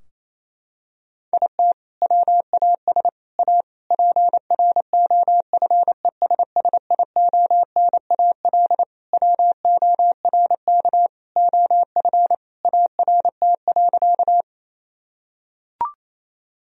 Morse code, heard as a keyed tone, about 28 wpm, 700 Hz. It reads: IT WAS A PROFESSIONAL WORK OF ART.
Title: Sentences from Top 1000 Words - Review Part 4 - 28wpm